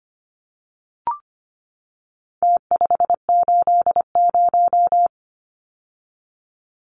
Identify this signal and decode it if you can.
Morse code, heard as a keyed tone, about 25 wpm, 700 Hz. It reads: T580
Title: Numbers - Contest Style - 4-Digit 25wpm